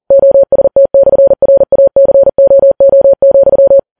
Morse code, heard as a keyed tone, about 40 wpm, 550 Hz. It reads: OST/RACOO,